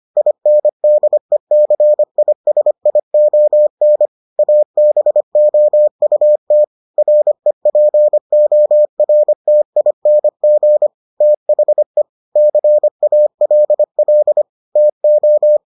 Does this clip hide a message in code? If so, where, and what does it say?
Yes: Morse code from 0.2 to 15.6 s, heard as a keyed tone, about 25 wpm, 600 Hz. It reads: INDECISION ABOUT REPORTING THE CALL TO